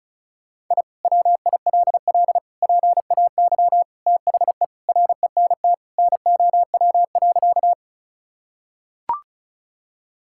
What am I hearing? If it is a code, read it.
Morse code, heard as a keyed tone, about 36 wpm, 700 Hz. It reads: I WILL PAY THE RENT NOW.
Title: Sentences from Top 1000 Words - Review Part 5 - 35wpm